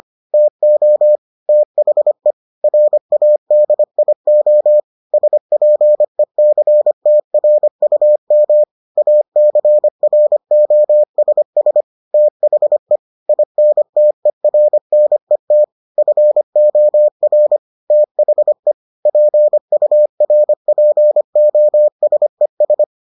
Morse code, heard as a keyed tone, about 25 wpm, 600 Hz. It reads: TO THE RADIO SPECTRUM ACROSS THE INTERNET FOR THE PURPOSES